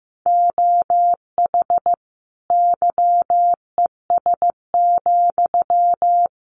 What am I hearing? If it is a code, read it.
Morse code, heard as a keyed tone, about 15 wpm, 700 Hz. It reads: OH YES,
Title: Sentences from Top 900 Words 15wpm